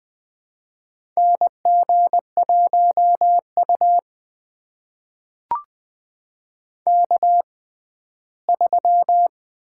Morse code, heard as a keyed tone, about 20 wpm, spaced as for 14 wpm, 700 Hz. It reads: NG1U K3
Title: USA Call Signs - 4 Characters - 6x Character Spacing - 20wpm